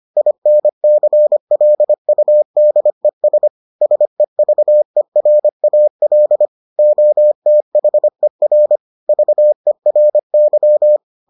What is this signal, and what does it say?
Morse code, heard as a keyed tone, about 25 wpm, 600 Hz. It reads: INCLUDES SEVERAL OTHER VERY